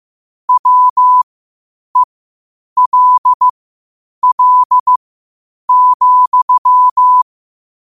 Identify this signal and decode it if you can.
Morse code, heard as a keyed tone, about 15 wpm, spaced as for 8 wpm, 1000 Hz. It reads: WELL,
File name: 8wpm